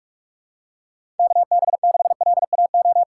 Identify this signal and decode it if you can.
Morse code, heard as a keyed tone, about 43 wpm, 700 Hz. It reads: KB6LAO
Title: USA Call Signs - 6 Characters - 6x Character Spacing - 45wpm